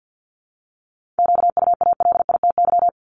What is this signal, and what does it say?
Morse code, heard as a keyed tone, about 50 wpm, 700 Hz. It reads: QUALITY
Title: Multi-letter Phonemes Repeated 3x 50wpm